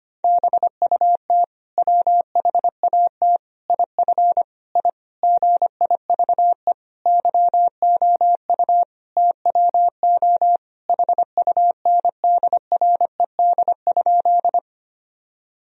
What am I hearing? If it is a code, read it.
Morse code, heard as a keyed tone, about 25 wpm, 700 Hz. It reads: BUT WHAT IF I GIVE YOU TWO HUNDRED?
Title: Sentences from Top 1000 Words - Review Part 2 - 25wpm